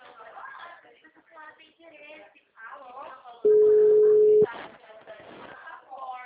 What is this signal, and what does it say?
Ringback tone (Europe, 425 Hz), with speech.